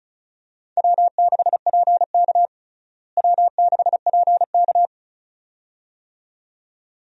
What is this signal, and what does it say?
Morse code, heard as a keyed tone, about 34 wpm, 700 Hz. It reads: W6PK W6PK
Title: USA Call Signs - 4 Characters - Repeated 2x - 35wpm